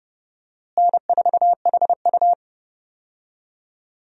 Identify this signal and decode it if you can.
Morse code, heard as a keyed tone, about 32 wpm, 700 Hz. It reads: N4HU
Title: USA Call Signs - 4 Characters - Repeated 3x - 30wpm